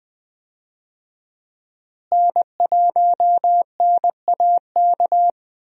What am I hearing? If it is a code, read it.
Morse code, heard as a keyed tone, about 21 wpm, 700 Hz. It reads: N1NAK
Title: USA Call Signs - 5 Characters - Repeated 2x - 20wpm